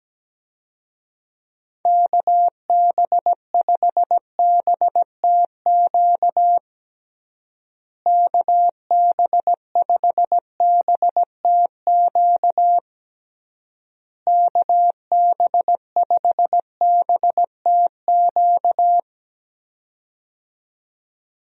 Morse code, heard as a keyed tone, about 17 wpm, 700 Hz. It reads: KB5BTQ KB5BTQ KB5BTQ
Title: USA Call Signs - 6 Characters - Repeated 3x - 17wpm